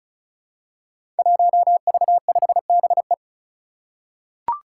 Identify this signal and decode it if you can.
Morse code, heard as a keyed tone, about 34 wpm, 700 Hz. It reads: 1V5BE
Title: Five Challenging Letter-Number Combinations 35wpm